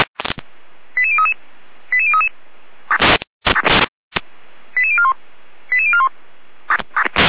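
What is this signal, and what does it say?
Five-tone selective call (ZVEI), heard as quick runs of five tones, tones of about 70 ms. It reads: 88033 88033 88041 88041
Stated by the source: ZVEI five-tone call 88189 received by SDR.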